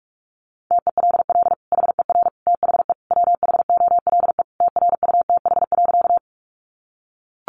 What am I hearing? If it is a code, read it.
Morse code, heard as a keyed tone, about 44 wpm, 700 Hz. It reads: TELL HER THE WHOLE TRUTH.
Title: Sentences from Top 800 Words 45wpm